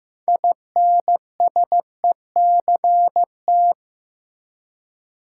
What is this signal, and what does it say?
Morse code, heard as a keyed tone, about 15 wpm, 700 Hz. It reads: INSECT
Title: Six Letter Words - 15wpm